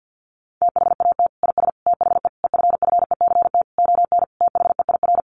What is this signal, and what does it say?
Morse code, heard as a keyed tone, about 50 wpm, 700 Hz. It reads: THAT IS THE EFFECT ON THEIR
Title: Sentences from Top 1300 Words 50wpm